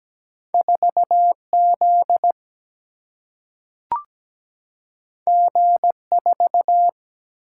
Morse code, heard as a keyed tone, about 17 wpm, 700 Hz. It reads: 4Z G4
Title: Single-Letter Single-Number 17wpm